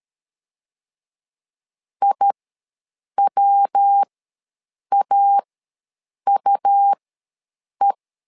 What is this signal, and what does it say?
Morse code, heard as a keyed tone, about 13 wpm, spaced as for 7 wpm, 800 Hz. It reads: IWAUE
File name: LEZIONI / 3 MWRO / 3 MWR ws 12wpm 1809